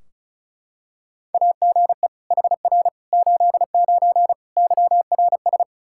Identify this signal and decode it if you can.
Morse code, heard as a keyed tone, about 34 wpm, 700 Hz. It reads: AGE HR 89 YRS
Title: Practice QSO - Exchange 3 - More Info - 35wpm